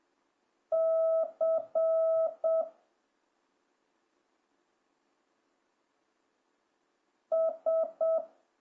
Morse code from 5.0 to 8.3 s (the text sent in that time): S